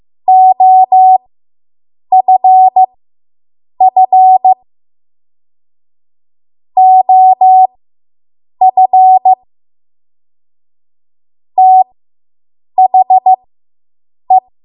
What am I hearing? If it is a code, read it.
Morse code, heard as a keyed tone, about 15 wpm, spaced as for 7 wpm, 750 Hz. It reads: OFF OF THE